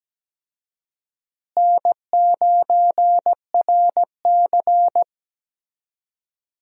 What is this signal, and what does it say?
Morse code, heard as a keyed tone, about 17 wpm, 700 Hz. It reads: N9RC
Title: USA Call Signs - 4 Characters - Repeated 3x - 17wpm